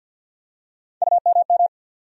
Morse code, heard as a keyed tone, about 48 wpm, 700 Hz. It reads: UMM
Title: USA Call Sign Postfixes - 3 Letters 50wpm